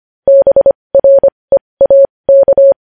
Morse code, heard as a keyed tone, about 25 wpm, spaced as for 20 wpm, 550 Hz. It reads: BREAK
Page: News headlines in Morse code at 20 WPM